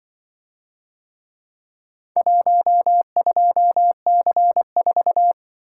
Morse code, heard as a keyed tone, about 24 wpm, 700 Hz. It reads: 12C4
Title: Speed-Racing - Four Letter-Number Combinations - 20wpm